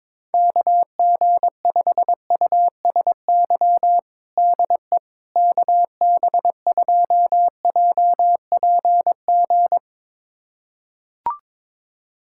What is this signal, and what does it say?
Morse code, heard as a keyed tone, about 22 wpm, 700 Hz. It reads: KG5USY DE KB2JPG
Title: USA Callsign DE Callsign - 6 Characters - 22wpm